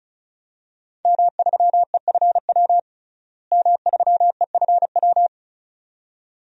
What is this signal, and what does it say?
Morse code, heard as a keyed tone, about 34 wpm, 700 Hz. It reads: M3EFW M3EFW
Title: DXCC-100 Call Signs - 5 Characters - Repeated 2x - 35wpm